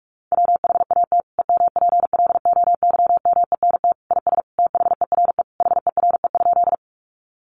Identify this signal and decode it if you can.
Morse code, heard as a keyed tone, about 44 wpm, 700 Hz. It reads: WHAT EMPLOYMENT IS THERE HERE?
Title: Sentences from Top 1500 Words 45wpm